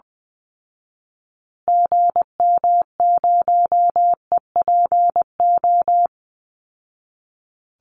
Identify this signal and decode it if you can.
Morse code, heard as a keyed tone, about 20 wpm, 700 Hz. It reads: GM0EPO
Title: Call Signs Repeated 2x - 20wpm